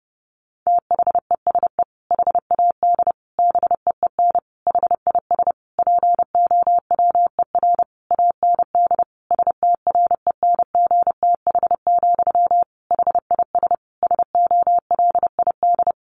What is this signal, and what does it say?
Morse code, heard as a keyed tone, about 30 wpm, 700 Hz. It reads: THESE HAD BEEN HIS POWER AND STRENGTH, HIS SOLID
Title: Sentences from Top 1500 Words 30wpm